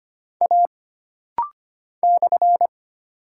Morse code, heard as a keyed tone, about 25 wpm, 700 Hz. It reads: A /
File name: Single Character (Letter:Number:Punctuation:Prosign) - Rapid-Fire - 25wpm